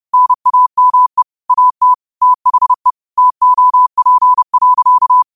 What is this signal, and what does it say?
Morse code, heard as a keyed tone, about 30 wpm, 1000 Hz. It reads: NAME AT THE TOP.
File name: qotdfiles / 30wpm